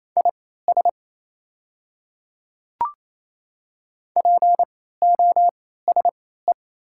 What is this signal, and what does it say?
Morse code, heard as a keyed tone, about 27 wpm, spaced as for 16 wpm, 690 Hz. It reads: IS POSE
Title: 3x Character Spacing - Top 500 Words - 28wpm